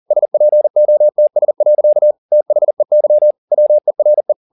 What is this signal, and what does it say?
Morse code, heard as a keyed tone, about 40 wpm, 600 Hz. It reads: SPOTS. THEY WERE
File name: the old man and the sea 2 40wpm